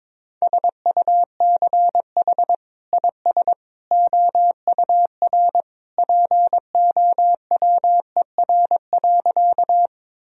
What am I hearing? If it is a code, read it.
Morse code, heard as a keyed tone, about 22 wpm, 700 Hz. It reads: SUCH IS OUR POWER.
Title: Sentences from Top 1000 Words - Review Part 2 - 22wpm